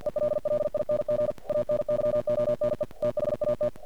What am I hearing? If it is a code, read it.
Morse code, heard as a keyed tone, about 50 wpm, 600 Hz, with noise. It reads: ELLING ANYONE THAT
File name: archived / CRSnet